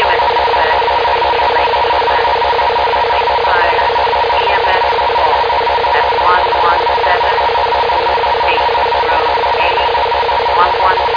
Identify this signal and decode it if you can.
Two-tone teleprinter signal, about 450 Hz shift.
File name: Indian Lake 2024 04 07 11 21 09